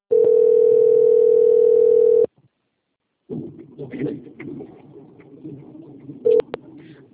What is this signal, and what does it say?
Ringback tone (North America), 440 + 480 Hz.